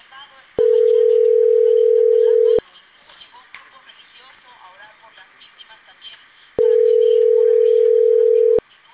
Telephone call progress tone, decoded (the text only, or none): ringback tone